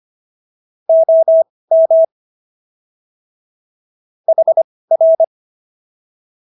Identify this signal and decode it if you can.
Morse code, heard as a keyed tone, about 25 wpm, spaced as for 18 wpm, 650 Hz. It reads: OM HR